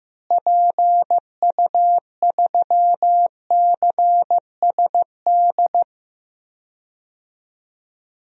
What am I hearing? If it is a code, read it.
Morse code, heard as a keyed tone, about 15 wpm, 700 Hz. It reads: PU3CSD
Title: DXCC-100 Call Signs - 6 Characters - Repeated 2x - 15wpm